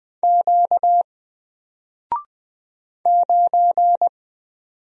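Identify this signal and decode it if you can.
Morse code, heard as a keyed tone, about 20 wpm, 700 Hz. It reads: Q 9
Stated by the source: Single letter and number Morse drill at 20 wpm, from N4LCD.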